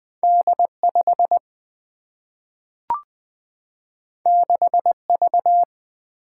Morse code, heard as a keyed tone, about 20 wpm, 700 Hz. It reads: D 5 6 V
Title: Single-Letter Single-Number 20wpm